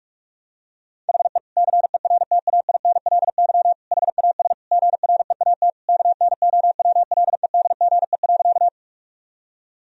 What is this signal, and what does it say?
Morse code, heard as a keyed tone, about 46 wpm, 700 Hz. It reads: HE CERTAINLY HAS GREAT KNOWLEDGE.
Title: Sentences from Top 1000 Words - Review Part 3 - 45wpm